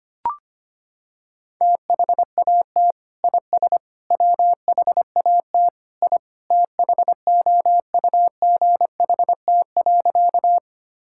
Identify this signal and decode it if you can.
Morse code, heard as a keyed tone, about 25 wpm, 700 Hz. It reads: THAT IS WHAT I THOUGHT.
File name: practice / 25wpm / Sentences with 2-Word Phrases - 25wpm